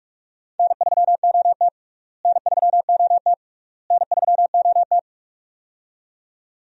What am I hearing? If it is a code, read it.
Morse code, heard as a keyed tone, about 46 wpm, 700 Hz. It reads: N3OT N3OT N3OT
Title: USA Call Signs - 4 Characters - Repeated 3x - 45wpm